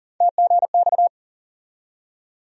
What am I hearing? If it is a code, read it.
Morse code, heard as a keyed tone, about 40 wpm, 700 Hz. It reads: TGX
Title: USA Call Sign Postfixes - 3 Letters 40wpm